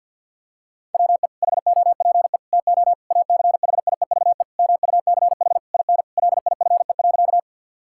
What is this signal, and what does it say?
Morse code, heard as a keyed tone, about 52 wpm, 700 Hz. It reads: WE HOPE TO ACHIEVE MUCH IN LIFE.
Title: Sentences from Top 1000 Words - Review Part 6 - 50wpm